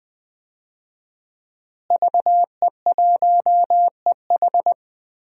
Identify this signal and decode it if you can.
Morse code, heard as a keyed tone, about 20 wpm, 700 Hz. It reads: VE1EH
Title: DXCC-100 Call Signs - 5 Characters - Repeated 2x - 20wpm